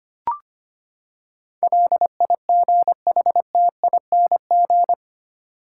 Morse code, heard as a keyed tone, about 25 wpm, 700 Hz. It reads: LIGHTING
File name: Eight Letter Words - 25wpm